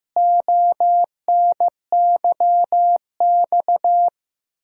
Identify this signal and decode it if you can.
Morse code, heard as a keyed tone, about 15 wpm, 700 Hz. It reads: ONYX
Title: Pangram Challenge v2 - 15wpm